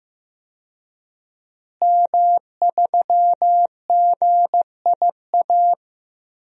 Morse code, heard as a keyed tone, about 15 wpm, 700 Hz. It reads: M3GIA
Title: DXCC-100 Call Signs - 5 Characters - Repeated 2x - 15wpm